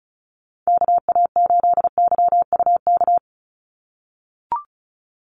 Morse code, heard as a keyed tone, about 34 wpm, 700 Hz. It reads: KA8YUK
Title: USA Call Signs 35wpm